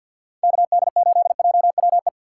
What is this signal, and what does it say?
Morse code, heard as a keyed tone, about 51 wpm, 690 Hz. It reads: KD8JWE
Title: USA Call Signs - 6 Characters - 3x Character Spacing - 50wpm